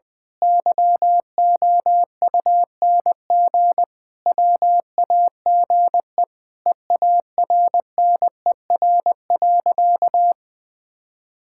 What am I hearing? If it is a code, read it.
Morse code, heard as a keyed tone, about 20 wpm, 700 Hz. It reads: YOUNG WAGE EARNER.
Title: Pangram Challenge v2 - 20wpm